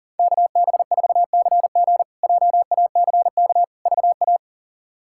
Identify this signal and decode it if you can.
Morse code, heard as a keyed tone, about 40 wpm, 700 Hz. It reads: KB4CG JACK VA
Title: K1USN SST QSO Practice - 40wpm